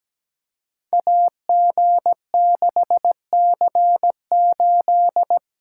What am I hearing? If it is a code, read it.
Morse code, heard as a keyed tone, about 17 wpm, 700 Hz. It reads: AG6C8
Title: Code Groups - 5 Letter:Numbers - 17wpm